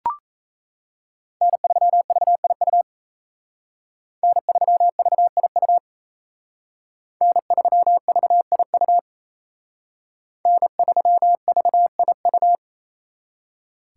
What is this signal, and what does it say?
Morse code, heard as a keyed tone, about 34 wpm, 700 Hz. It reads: N3VIU N3VIU N3VIU N3VIU
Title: Speed Racing - USA 1x3 Call Signs 28wpm